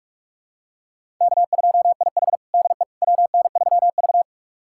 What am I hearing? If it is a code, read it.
Morse code, heard as a keyed tone, about 41 wpm, 700 Hz. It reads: K2IH DE WN3V
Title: USA Callsign DE Callsign - 4 Characters - 45wpm